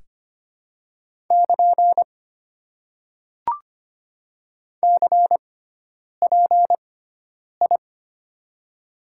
Morse code, heard as a keyed tone, about 25 wpm, 700 Hz. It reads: <KN> C P I
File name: Top 90 QSO Elements - 6x Character Spacing - 25wpm